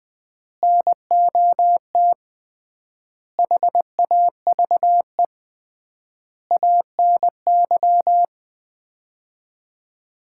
Morse code, heard as a keyed tone, about 21 wpm, 700 Hz. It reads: NOT HAVE ANY
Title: 3x Word-Spacing - Sets of 5 Words - 20wpm